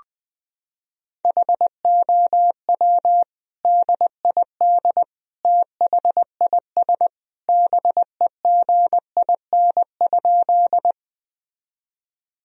Morse code, heard as a keyed tone, about 21 wpm, 700 Hz. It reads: HOW DID THIS BEGIN?
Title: Sentences from Top 1000 Words - Review Part 1 - 20wpm